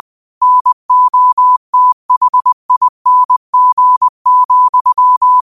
Morse code, heard as a keyed tone, about 20 wpm, 1000 Hz. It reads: NOTHING,